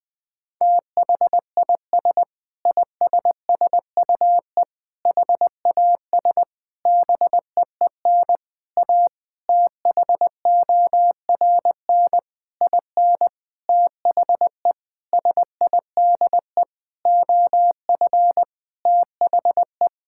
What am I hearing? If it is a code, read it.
Morse code, heard as a keyed tone, about 21 wpm, 700 Hz. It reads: THIS ISSUE HAS BEEN A THORN IN THE SIDE OF THE
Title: Sentences with 8-Word Phrases - Spoken Beginning - 20wpm